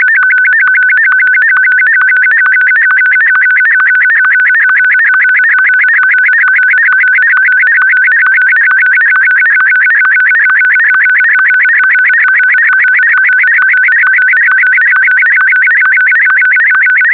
SSTV (slow-scan television), Martin 1.